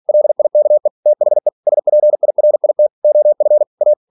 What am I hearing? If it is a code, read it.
Morse code, heard as a keyed tone, about 46 wpm, 600 Hz. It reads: LIKE THE SPIRIT OF A
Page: Never 47wpm